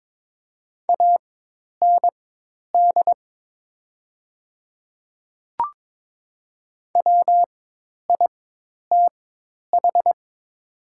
Morse code, heard as a keyed tone, about 22 wpm, spaced as for 10 wpm, 700 Hz. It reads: AND WITH
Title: 4x Character Spacing - Top 200 Words - 22wpm